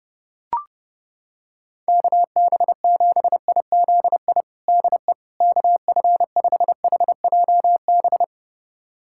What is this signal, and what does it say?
Morse code, heard as a keyed tone, about 32 wpm, 700 Hz. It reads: KB7IZI DE KF5HJB
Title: USA Callsign DE Callsign - 6 Characters - 30wpm